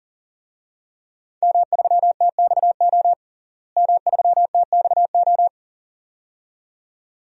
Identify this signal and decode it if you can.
Morse code, heard as a keyed tone, about 40 wpm, 700 Hz. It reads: M3TXO M3TXO
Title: DXCC-100 Call Signs - 5 Characters - Repeated 2x - 40wpm